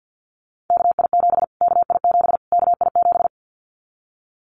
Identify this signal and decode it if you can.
Morse code, heard as a keyed tone, about 50 wpm, 700 Hz. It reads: KI7 KI7 KI7
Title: Callsign Prefixes for Top 25 DXCC - Repeated 3x - 50wpm